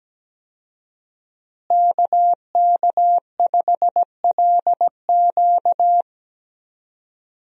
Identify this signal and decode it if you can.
Morse code, heard as a keyed tone, about 17 wpm, 700 Hz. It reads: KK5LQ
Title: USA Call Signs - 5 Characters - Repeated 2x - 17wpm